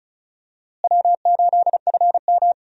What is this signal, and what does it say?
Morse code, heard as a keyed tone, about 36 wpm, 700 Hz. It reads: W8FM
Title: USA Call Signs - 4 Characters - Repeated 2x - 35wpm